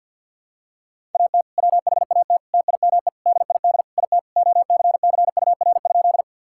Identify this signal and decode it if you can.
Morse code, heard as a keyed tone, about 48 wpm, 700 Hz. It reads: AT WHAT TIME DID IT OCCUR?
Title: Sentences from Top 1000 Words - Review Part 6 - 50wpm